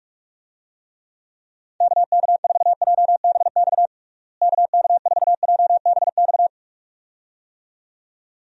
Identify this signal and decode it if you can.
Morse code, heard as a keyed tone, about 46 wpm, 700 Hz. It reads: KK4JBX KK4JBX
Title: USA Call Signs - 6 Characters - Repeated 2x - 45wpm